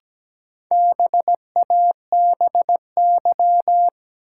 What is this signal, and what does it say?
Morse code, heard as a keyed tone, about 17 wpm, 700 Hz. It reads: BABY
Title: Sentences from Top 1300 Words - New Word Drill Alt - 17wpm